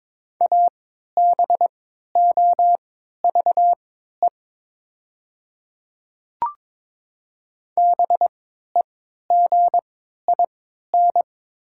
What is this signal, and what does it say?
Morse code, heard as a keyed tone, about 22 wpm, spaced as for 13 wpm, 700 Hz. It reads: ABOVE BEGIN
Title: 3x Character Spacing - Top 500 Words - 22wpm